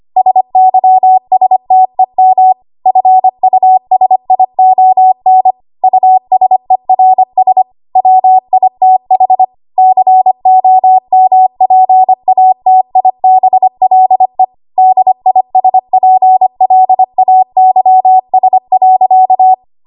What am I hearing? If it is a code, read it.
Morse code, heard as a keyed tone, about 25 wpm, 750 Hz. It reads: SYSTEM FUSION USERS WITH COMPATIBLE DISPLAYS.
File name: Archive / 25 WPM / 25WPM